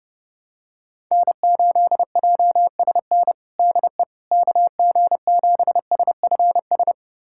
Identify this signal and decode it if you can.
Morse code, heard as a keyed tone, about 30 wpm, 700 Hz. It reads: N8JSN DE KG7SFS
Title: USA Callsign DE Callsign 30wpm